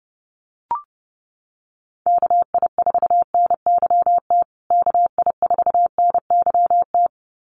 Morse code, heard as a keyed tone, about 30 wpm, 700 Hz. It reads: KI4NYT KI4NYT
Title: USA Call Signs Repeated 2x 30wpm